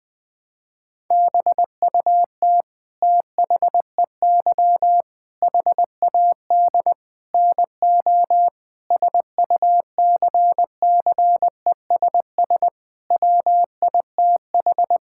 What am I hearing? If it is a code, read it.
Morse code, heard as a keyed tone, about 21 wpm, 700 Hz. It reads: BUT THEY HAD NO SUCCESS WITH
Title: Sentences from Top 1000 Words - Review Part 6 - 20wpm